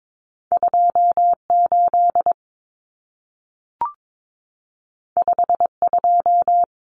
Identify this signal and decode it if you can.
Morse code, heard as a keyed tone, about 22 wpm, 700 Hz. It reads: 28 52
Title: Numbers - 2-Digit 22wpm